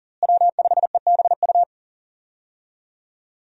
Morse code, heard as a keyed tone, about 37 wpm, 700 Hz. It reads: W5EBU
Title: USA Call Signs - 5 Characters - Repeated 2x - 40wpm